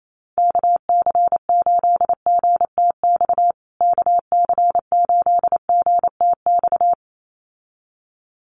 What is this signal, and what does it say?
Morse code, heard as a keyed tone, about 28 wpm, 700 Hz. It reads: KC8GTX KC8GTX
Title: USA Call Signs Repeated 2x 28wpm